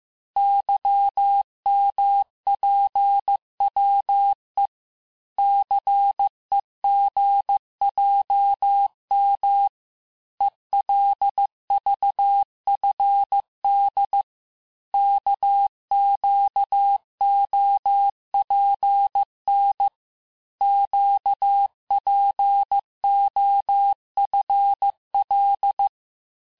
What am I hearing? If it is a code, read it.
Morse code, heard as a keyed tone, about 15 wpm, 800 Hz. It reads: YMPWE CEGJM ELVFD KQOPN QPOFL
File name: cours-du-lundi files / cours-1585-texte-chiffre-vitesse-14mpm